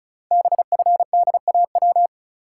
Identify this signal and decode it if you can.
Morse code, heard as a keyed tone, about 36 wpm, 700 Hz. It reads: BFDAW
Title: Code Groups - 5 Letters - 35wpm